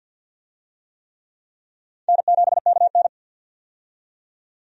Morse code, heard as a keyed tone, about 48 wpm, 700 Hz. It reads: N7KN
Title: USA Call Signs - 4 Characters - 50wpm